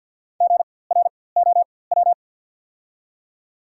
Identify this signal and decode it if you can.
Morse code, heard as a keyed tone, about 48 wpm, spaced as for 23 wpm, 700 Hz. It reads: GROW
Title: 4x Character Spacing - Top 300 Words - 50wpm